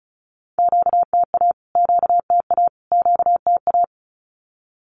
Morse code, heard as a keyed tone, about 34 wpm, 700 Hz. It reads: QTA QTA QTA
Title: Q-Codes Repeated 3x 35wpm